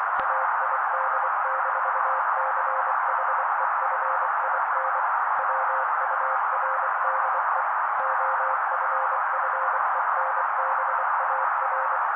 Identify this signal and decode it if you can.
Morse code, heard as a keyed tone, about 24 wpm, 530 Hz. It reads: AID-CHEFIN WURDE OFFENBAR
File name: daily / 23wpm noise